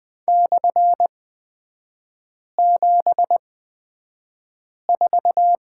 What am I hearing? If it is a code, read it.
Morse code, heard as a keyed tone, about 20 wpm, spaced as for 5 wpm, 700 Hz. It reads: /74
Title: 229 Lesson 37e - Letter x - Three Characters 20wpm